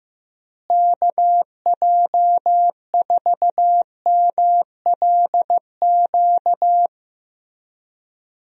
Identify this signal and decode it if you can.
Morse code, heard as a keyed tone, about 15 wpm, 700 Hz. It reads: KJ4MLQ
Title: USA Call Signs - 6 Characters - Repeated 3x - 15wpm